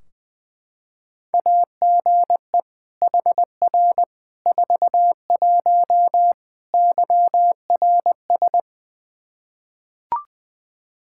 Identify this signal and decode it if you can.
Morse code, heard as a keyed tone, about 20 wpm, 700 Hz. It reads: AGE HR 41 YRS
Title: Practice QSO - Exchange 3 - More Info - 20wpm